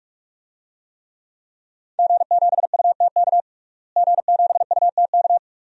Morse code, heard as a keyed tone, about 42 wpm, 700 Hz. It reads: G7UTK G7UTK
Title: DXCC-100 Call Signs - 5 Characters - Repeated 2x - 45wpm